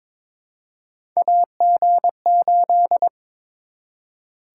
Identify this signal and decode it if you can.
Morse code, heard as a keyed tone, about 22 wpm, 700 Hz. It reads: AG8
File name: practice / 22wpm / Three Letter-Number Combinations - 22wpm